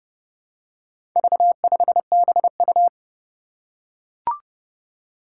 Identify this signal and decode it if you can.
Morse code, heard as a keyed tone, about 30 wpm, 700 Hz. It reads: V5BU
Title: Four Challenging Letter-Number Combinations 30wpm